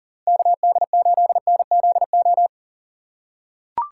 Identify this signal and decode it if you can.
Morse code, heard as a keyed tone, about 40 wpm, 700 Hz. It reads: KD8NZO
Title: USA Call Signs - 6 Characters - 3x Character Spacing - 40wpm